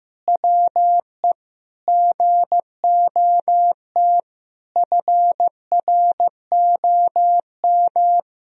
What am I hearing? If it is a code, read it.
Morse code, heard as a keyed tone, about 15 wpm, 700 Hz. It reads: WE GOT FROM
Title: Sentences from Top 1000 Words - Review Part 2 - 15wpm